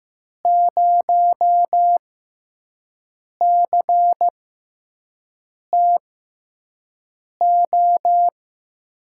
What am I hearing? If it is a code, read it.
Morse code, heard as a keyed tone, about 15 wpm, spaced as for 5 wpm, 700 Hz. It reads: 0CTO
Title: USA Call Signs - 6 Characters - 6x Character Spacing - 15wpm